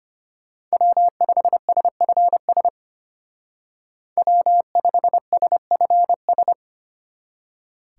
Morse code, heard as a keyed tone, about 33 wpm, 700 Hz. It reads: W5SFS W5SFS